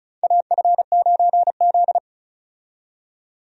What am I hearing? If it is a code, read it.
Morse code, heard as a keyed tone, about 34 wpm, 700 Hz. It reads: AF9Z
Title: USA Call Signs - 4 Characters - 35wpm